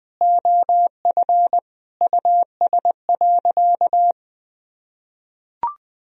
Morse code, heard as a keyed tone, about 20 wpm, 700 Hz. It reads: OF US.
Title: Sentences from Top 1000 Words - Review Part 1 - 20wpm